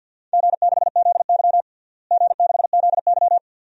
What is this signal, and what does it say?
Morse code, heard as a keyed tone, about 53 wpm, 700 Hz. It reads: G6ZY G6ZY
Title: DXCC-100 Call Signs - 4 Characters - Repeated 2x - 50wpm